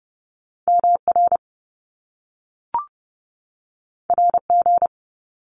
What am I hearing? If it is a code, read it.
Morse code, heard as a keyed tone, about 30 wpm, 700 Hz. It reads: MR RG
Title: Top Two Letter Combinations in Words 30wpm